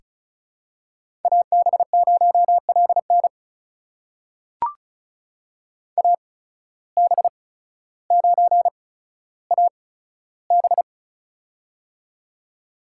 Morse code, heard as a keyed tone, about 36 wpm, spaced as for 15 wpm, 700 Hz. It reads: AB0LN AB9AB
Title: USA Call Signs - 5 Characters - 8x Character Spacing - 35wpm